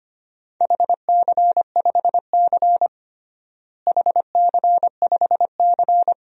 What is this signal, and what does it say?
Morse code, heard as a keyed tone, about 25 wpm, 700 Hz. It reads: HC5C HC5C
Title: DXCC-100 Call Signs - 4 Characters - Repeated 3x - 25wpm